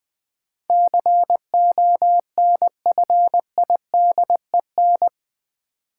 Morse code, heard as a keyed tone, about 20 wpm, 700 Hz. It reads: CONFIDEN
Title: Build A Word - 4 Letters At A Time - 10-Letter Words - 20wpm